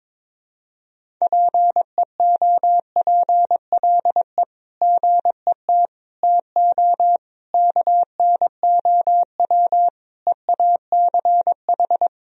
Morse code, heard as a keyed tone, about 21 wpm, 700 Hz. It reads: PEOPLE GET TO KNOW EACH
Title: Sentences from Top 1000 Words - Review Part 1 - 22wpm